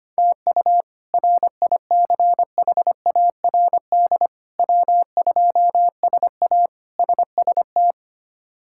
Morse code, heard as a keyed tone, about 25 wpm, 700 Hz. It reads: TU RICHARD W2SA SST
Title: K1USN SST QSO Practice - 25wpm